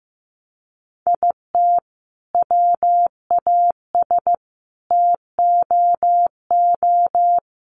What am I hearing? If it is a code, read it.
Morse code, heard as a keyed tone, about 15 wpm, 700 Hz. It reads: IT WAS TOO
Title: Sentences from Top 1200 Words 15wpm